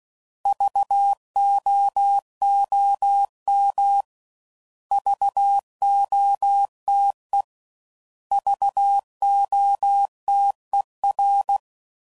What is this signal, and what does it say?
Morse code, heard as a keyed tone, about 16 wpm, 800 Hz. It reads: VOOM VOTE VOTER
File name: Prefixes / By First Letter / Vs